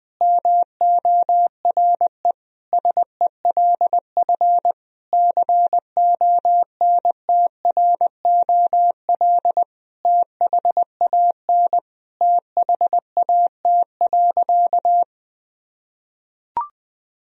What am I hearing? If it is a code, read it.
Morse code, heard as a keyed tone, about 20 wpm, 700 Hz. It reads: MORE SELF CONTROL THAN THAT.
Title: Sentences from Top 1000 Words - Review Part 3 - 20wpm